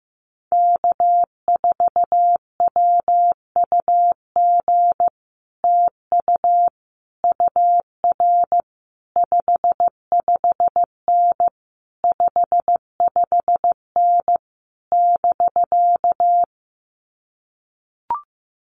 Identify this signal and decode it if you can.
Morse code, heard as a keyed tone, about 15 wpm, 700 Hz. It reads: K4WUG TU UR 55N 55N <BK>
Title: Common QSO Phrases Encore 15wpm